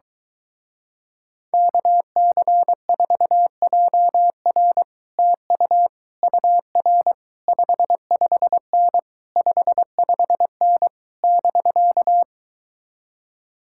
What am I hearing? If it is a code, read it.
Morse code, heard as a keyed tone, about 24 wpm, 700 Hz. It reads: KC4JR TU UR 55N 55N <BK>